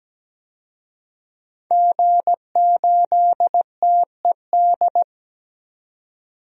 Morse code, heard as a keyed tone, about 17 wpm, 700 Hz. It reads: G8TED
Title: DXCC-100 Call Signs - 5 Characters - 17wpm